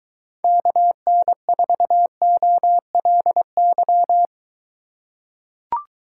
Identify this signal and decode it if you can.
Morse code, heard as a keyed tone, about 23 wpm, 700 Hz. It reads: KN4OLY